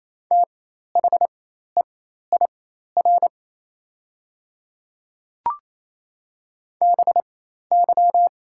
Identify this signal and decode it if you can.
Morse code, heard as a keyed tone, about 28 wpm, spaced as for 13 wpm, 700 Hz. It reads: THEIR BY
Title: 4x Character Spacing - Top 200 Words - 28wpm